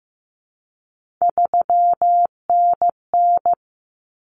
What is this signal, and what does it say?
Morse code, heard as a keyed tone, about 15 wpm, 700 Hz. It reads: 3NN
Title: Common Cut Numbers - 3-Digit 15wpm